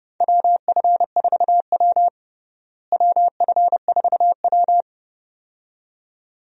Morse code, heard as a keyed tone, about 30 wpm, 700 Hz. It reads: WF4W WF4W
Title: USA Call Signs - 4 Characters - Repeated 2x - 30wpm